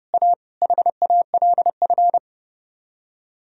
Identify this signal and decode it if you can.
Morse code, heard as a keyed tone, about 28 wpm, 700 Hz. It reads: A HALF
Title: Speed-Racing - Sets of 3 Words 30wpm